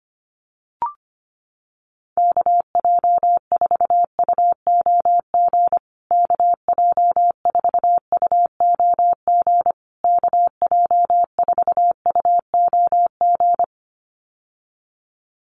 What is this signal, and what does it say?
Morse code, heard as a keyed tone, about 25 wpm, 700 Hz. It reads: KJ4UOG KJ4UOG KJ4UOG
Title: USA Call Signs Repeated 3x 25wpm